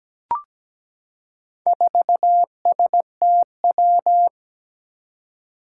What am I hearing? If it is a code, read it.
Morse code, heard as a keyed tone, about 17 wpm, 700 Hz. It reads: 4STW
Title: Four Challenging Letter-Number Combinations 17wpm